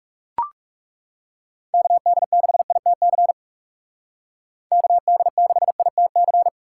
Morse code, heard as a keyed tone, about 37 wpm, 700 Hz. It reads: KD6ITC KD6ITC
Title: Speed Racing - USA Call Signs 30wpm